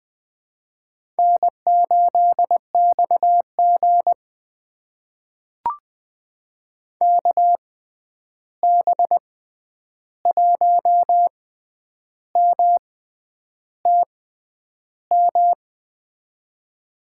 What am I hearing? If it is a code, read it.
Morse code, heard as a keyed tone, about 20 wpm, spaced as for 9 wpm, 700 Hz. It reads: N8XG KB1MTM
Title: USA Call Signs - 6x Character Spacing - 20wpm